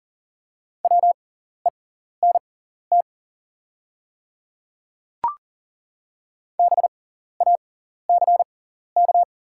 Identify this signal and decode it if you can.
Morse code, heard as a keyed tone, about 40 wpm, spaced as for 14 wpm, 700 Hz. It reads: WENT BACK
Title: 6x Character Spacing - Top 300 Words - 40wpm